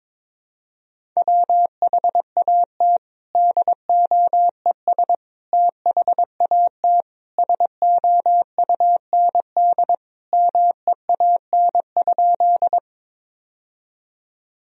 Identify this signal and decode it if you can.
Morse code, heard as a keyed tone, about 22 wpm, 700 Hz. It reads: WHAT DOES THAT SOUND MEAN?
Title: Sentences from Top 1000 Words - Review Part 1 - 22wpm